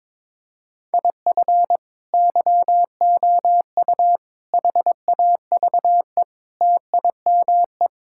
Morse code, heard as a keyed tone, about 22 wpm, 700 Hz. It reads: IF YOU HAVE TIME
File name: practice / 22wpm / Sentences with 3-Word Phrases - 22wpm